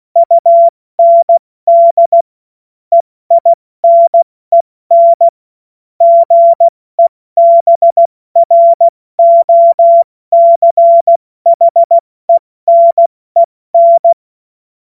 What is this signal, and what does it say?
Morse code, heard as a keyed tone, about 16 wpm, 680 Hz. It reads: UND EINEN GEBROCHENEN
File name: Morsekurs / Witze / Witz 21. September 2021, 70 F80